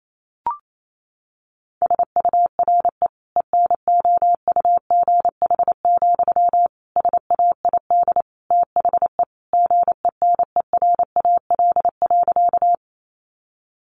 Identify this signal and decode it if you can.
Morse code, heard as a keyed tone, about 28 wpm, 700 Hz. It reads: SURE ENOUGH, SAID THE GENERAL.
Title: Sentences from Top 600 Words 28wpm